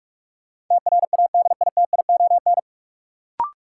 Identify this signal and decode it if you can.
Morse code, heard as a keyed tone, about 44 wpm, 700 Hz. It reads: TRADITION
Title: Sentences from Top 1500 Words - New Word Drill Alt - 45wpm